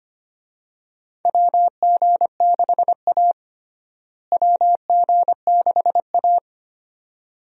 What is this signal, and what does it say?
Morse code, heard as a keyed tone, about 25 wpm, 700 Hz. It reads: WG6A WG6A
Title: USA Call Signs - 4 Characters - Repeated 2x - 25wpm